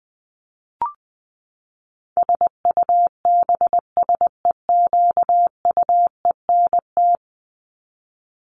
Morse code, heard as a keyed tone, about 20 wpm, 700 Hz. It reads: SUBSEQUENT
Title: LY Suffix - Top 100 Words with Base Words 20wpm